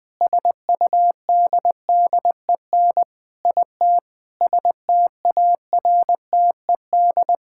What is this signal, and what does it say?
Morse code, heard as a keyed tone, about 20 wpm, 700 Hz. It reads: SUDDEN IT STARTED
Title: Sentences with 4-Word Phrases - 20wpm